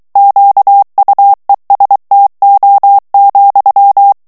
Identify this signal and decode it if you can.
Morse code, heard as a keyed tone, about 23 wpm, 800 Hz. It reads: QUESTO,
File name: ITALIEN texte clair 2 vitesse 22 mots minute